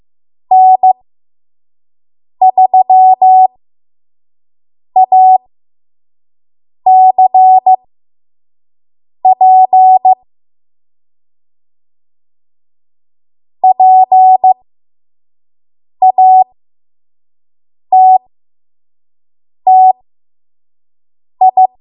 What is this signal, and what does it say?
Morse code, heard as a keyed tone, about 15 wpm, spaced as for 5 wpm, 750 Hz. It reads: N3ACP PATTI
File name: Morse / 05WPM